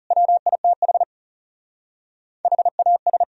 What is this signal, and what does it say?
Morse code, heard as a keyed tone, about 34 wpm, 700 Hz. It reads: WITH HAS
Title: Speed-Racing - Sets of 5 Words 30wpm